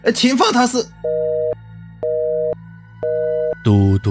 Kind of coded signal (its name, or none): telephone call progress tone